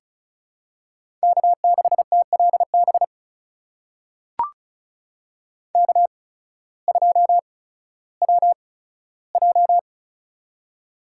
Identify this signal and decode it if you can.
Morse code, heard as a keyed tone, about 36 wpm, spaced as for 16 wpm, 700 Hz. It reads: K6TLB K2WJ
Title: USA Call Signs - 8x Character Spacing - 35wpm